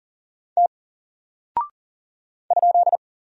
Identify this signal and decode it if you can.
Morse code, heard as a keyed tone, about 39 wpm, 700 Hz. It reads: T ?
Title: Single Character (Letter:Number:Punctuation:Prosign) 40wpm